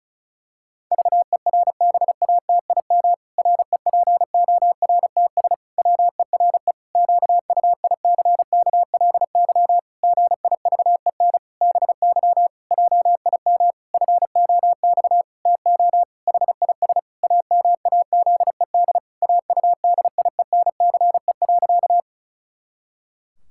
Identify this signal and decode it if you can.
Morse code, heard as a keyed tone, about 34 wpm, 700 Hz. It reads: VERBATIM REPORTS WERE QUICKLY GIVEN BY JIM FOX TO HIS AMAZED AUDIENCE.
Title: Pangram Challenge v2 - 35wpm